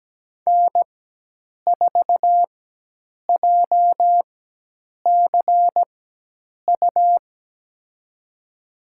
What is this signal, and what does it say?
Morse code, heard as a keyed tone, about 17 wpm, spaced as for 8 wpm, 700 Hz. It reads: N4JCU
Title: USA Call Signs - 5 Characters - 4x Character Spacing - 17wpm